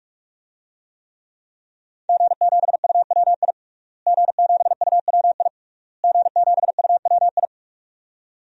Morse code, heard as a keyed tone, about 46 wpm, 700 Hz. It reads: G7UWI G7UWI G7UWI
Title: DXCC-100 Call Signs - 5 Characters - Repeated 3x - 45wpm